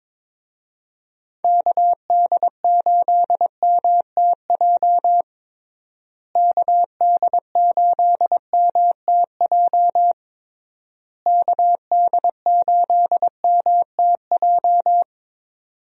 Morse code, heard as a keyed tone, about 22 wpm, 700 Hz. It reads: KD8MTJ KD8MTJ KD8MTJ
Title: USA Call Signs - 6 Characters - Repeated 3x - 22wpm